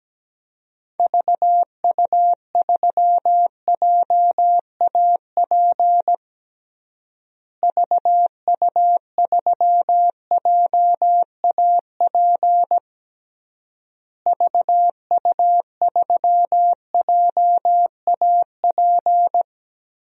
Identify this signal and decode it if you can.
Morse code, heard as a keyed tone, about 17 wpm, 700 Hz. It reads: VU3JAP VU3JAP VU3JAP
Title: DXCC-100 Call Signs - 6 Characters - Repeated 3x - 17wpm